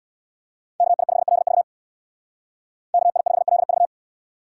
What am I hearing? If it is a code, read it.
Morse code, heard as a keyed tone, about 77 wpm, 700 Hz. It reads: KI5LV KI5LV